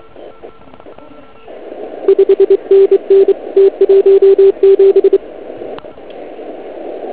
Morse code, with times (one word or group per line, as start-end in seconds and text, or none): 2.1-3.3 5NN
3.6-5.2 T17